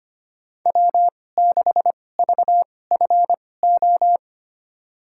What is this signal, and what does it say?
Morse code, heard as a keyed tone, about 25 wpm, spaced as for 18 wpm, 700 Hz. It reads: W6VFO
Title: USA Call Signs - 5 Characters - 2x Character Spacing - 25wpm